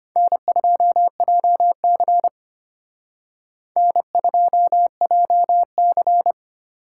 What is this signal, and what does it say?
Morse code, heard as a keyed tone, about 29 wpm, 700 Hz. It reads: N2JC N2JC
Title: Speed Racing - USA 1x2 Call Signs 25wpm